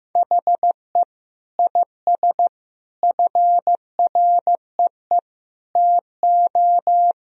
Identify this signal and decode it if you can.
Morse code, heard as a keyed tone, about 16 wpm, 700 Hz. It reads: HE IS FREE TO
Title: Sentences from Top 1000 Words - Review Part 4 - 15wpm